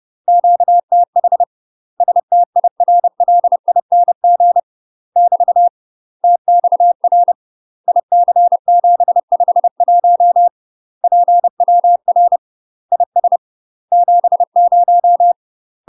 Morse code, heard as a keyed tone, about 30 wpm, 700 Hz. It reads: QTH STIRLING <BT> TXR IC751 PWR IS 70
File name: UK MASTER QSOs TRAINING / 30WPM / UK MASTER QSOs 0029